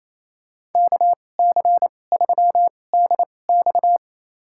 Morse code, heard as a keyed tone, about 28 wpm, spaced as for 20 wpm, 700 Hz. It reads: KC3DX
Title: USA Call Signs - 5 Characters - 2x Character Spacing - 28wpm